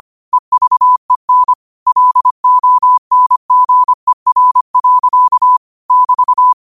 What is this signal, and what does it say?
Morse code, heard as a keyed tone, about 25 wpm, 1000 Hz. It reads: EVEN LONGER. <BT>
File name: qotdfiles / 25wpm